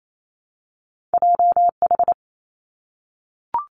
Morse code, heard as a keyed tone, about 28 wpm, 700 Hz. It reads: J H
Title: Two Challenging Letter-Number Combinations 28wpm